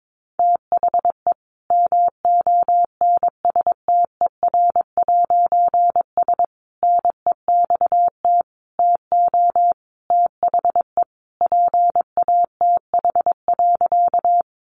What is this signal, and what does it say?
Morse code, heard as a keyed tone, about 22 wpm, 700 Hz. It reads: THE MONSTER'S NEXT TO THE PATH.
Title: A Sound Of Thunder 22wpm